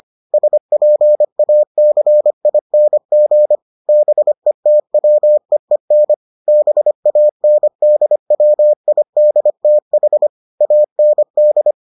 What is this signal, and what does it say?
Morse code, heard as a keyed tone, about 25 wpm, 600 Hz. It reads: SPACING BETWEEN BANDWIDTH AND